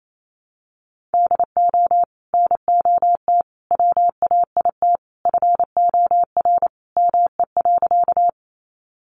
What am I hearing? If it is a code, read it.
Morse code, heard as a keyed tone, about 28 wpm, 700 Hz. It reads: DO NOT WAIT FOR ME.
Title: Sentences from Top 600 Words 28wpm